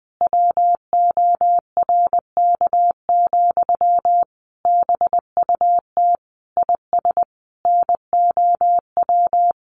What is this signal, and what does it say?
Morse code, heard as a keyed tone, about 20 wpm, 700 Hz. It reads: WORK, BUT IS NOW
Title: Sentences from Top 1500 Words 20wpm